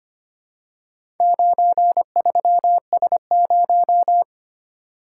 Morse code, heard as a keyed tone, about 25 wpm, 700 Hz. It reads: 93S0
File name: practice / 25wpm / Four Letter-Number Combinations - 25wpm